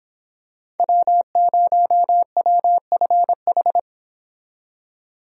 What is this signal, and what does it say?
Morse code, heard as a keyed tone, about 26 wpm, 700 Hz. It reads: W0WFH